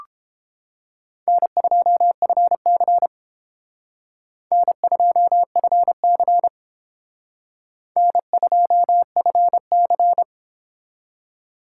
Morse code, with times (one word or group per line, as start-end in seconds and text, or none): 1.3-3.1 N2FC
4.5-6.5 N2FC
8.0-10.2 N2FC